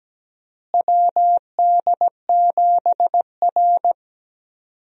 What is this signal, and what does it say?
Morse code, heard as a keyed tone, about 17 wpm, 700 Hz. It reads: WD7R
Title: USA Call Signs - 4 Characters - Repeated 3x - 17wpm